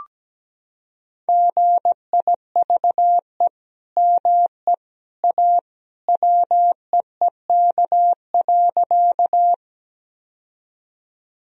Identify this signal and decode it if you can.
Morse code, heard as a keyed tone, about 17 wpm, 700 Hz. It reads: GIVE ME A WEEK.
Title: Sentences from Top 1000 Words - Review Part 2 - 17wpm